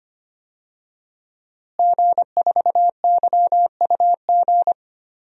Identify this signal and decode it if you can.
Morse code, heard as a keyed tone, about 25 wpm, 700 Hz. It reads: G4YUG
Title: DXCC-100 Call Signs - 5 Characters - Repeated 3x - 25wpm